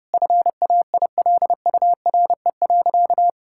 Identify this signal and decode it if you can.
Morse code, heard as a keyed tone, about 30 wpm, 700 Hz. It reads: FAILURE.
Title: Sentences from Top 1000 Words - Review Part 6 - 30wpm